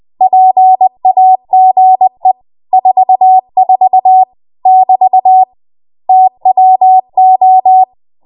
Morse code, heard as a keyed tone, about 20 wpm, 750 Hz. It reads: PAGE 44 <BT> TWO
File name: Morse / 20WPM